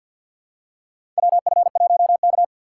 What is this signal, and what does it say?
Morse code, heard as a keyed tone, about 52 wpm, 700 Hz. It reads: WF1K
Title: USA Call Signs - 4x Character Spacing - 50wpm